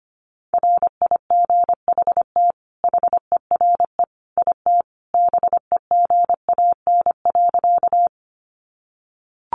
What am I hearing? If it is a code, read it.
Morse code, heard as a keyed tone, about 25 wpm, 700 Hz. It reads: RIGHT HERE IT BEGAN.
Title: Sentences from Top 300 Words 25wpm